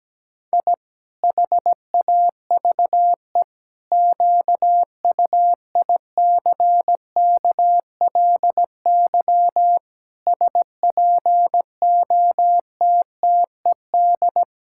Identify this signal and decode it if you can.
Morse code, heard as a keyed tone, about 17 wpm, 700 Hz. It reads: I HAVE QUICKLY SPOTTED
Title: Pangram Challenge v2 - 17wpm